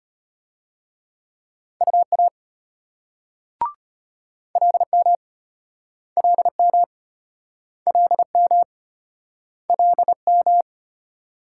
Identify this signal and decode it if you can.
Morse code, heard as a keyed tone, about 35 wpm, 700 Hz. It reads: UA LM LM LM LM